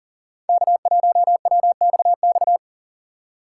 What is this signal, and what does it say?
Morse code, heard as a keyed tone, about 40 wpm, 700 Hz. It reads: K1WXX
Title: USA Call Signs - 4x Character Spacing - 40wpm